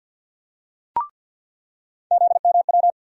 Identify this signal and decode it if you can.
Morse code, heard as a keyed tone, about 50 wpm, 700 Hz. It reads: ZMW
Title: USA Call Sign Postfixes - 3 Letters 50wpm